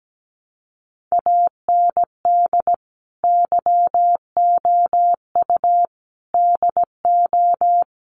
Morse code, heard as a keyed tone, about 17 wpm, 700 Hz. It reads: AND YOU DO
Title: Sets of 5 Words Encore2 17wpm